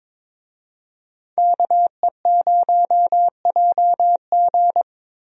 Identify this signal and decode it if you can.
Morse code, heard as a keyed tone, about 22 wpm, 700 Hz. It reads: KE0JG
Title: USA Call Signs - 5 Characters - Repeated 3x - 22wpm